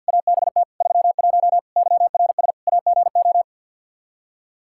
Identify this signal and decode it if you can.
Morse code, heard as a keyed tone, about 53 wpm, 700 Hz. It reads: ABT 31 YRS AGO
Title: Practice QSO - Exchange 3 - More Info - 50wpm